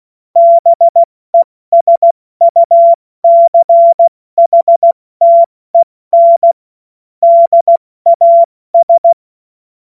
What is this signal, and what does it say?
Morse code, heard as a keyed tone, about 16 wpm, 680 Hz. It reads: BESUCHTEN DAS
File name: Morsekurs / Witze / Witz 23. August 2022 70 F80